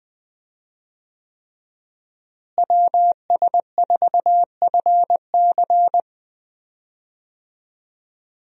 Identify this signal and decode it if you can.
Morse code, heard as a keyed tone, about 20 wpm, 700 Hz. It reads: WS4FC
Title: USA Call Signs - 5 Characters - Repeated 3x - 20wpm